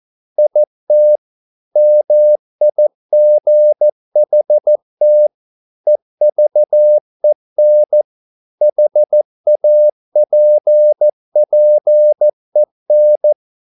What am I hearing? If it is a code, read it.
Morse code, heard as a keyed tone, about 14 wpm, 600 Hz. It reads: IT MIGHT EVEN HAPPEN